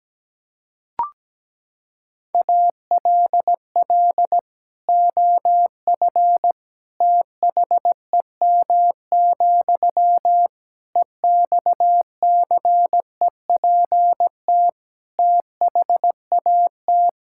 Morse code, heard as a keyed tone, about 17 wpm, 700 Hz. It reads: ALL OF THEM, EXCEPT THAT
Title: Sentences from Top 1600 Words 17wpm